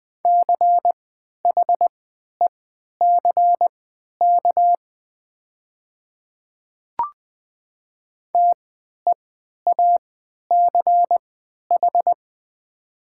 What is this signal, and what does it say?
Morse code, heard as a keyed tone, about 20 wpm, spaced as for 11 wpm, 700 Hz. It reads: CHECK TEACH
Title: 3x Character Spacing - Top 500 Words - 20wpm